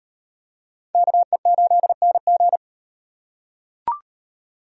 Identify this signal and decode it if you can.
Morse code, heard as a keyed tone, about 37 wpm, 700 Hz. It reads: KE8NG